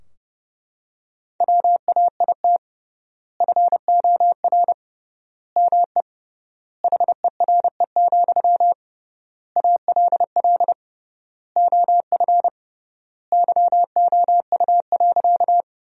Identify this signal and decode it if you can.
Morse code, heard as a keyed tone, about 32 wpm, 700 Hz. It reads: WAIT FOR ME HERE, ALL OF YOU.
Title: 3x Word-Spacing - Sentences from Top 500 Words - 30wpm